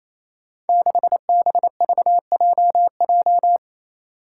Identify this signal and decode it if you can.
Morse code, heard as a keyed tone, about 27 wpm, 700 Hz. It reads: 6BVJJ
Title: Five Challenging Letter-Number Combinations 28wpm